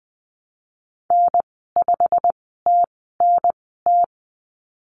Morse code, heard as a keyed tone, about 19 wpm, spaced as for 14 wpm, 700 Hz. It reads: N5TNT
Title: 088 Lesson 15g - Number 5 - Call Signs 20wpm